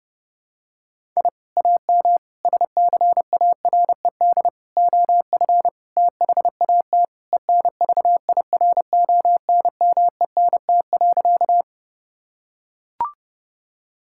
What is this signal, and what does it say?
Morse code, heard as a keyed tone, about 30 wpm, 700 Hz. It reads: I AM SCARED OF THAT ENVIRONMENT.
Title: Sentences from Top 1000 Words - Review Part 5 - 30wpm